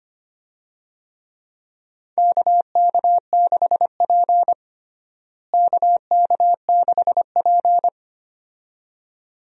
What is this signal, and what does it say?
Morse code, heard as a keyed tone, about 25 wpm, 700 Hz. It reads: KK6P KK6P
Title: USA Call Signs - 4 Characters - Repeated 2x - 25wpm